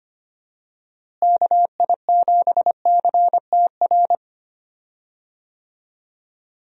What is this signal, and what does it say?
Morse code, heard as a keyed tone, about 25 wpm, 700 Hz. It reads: KI7CTR
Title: USA Call Signs - 6 Characters - Repeated 2x - 25wpm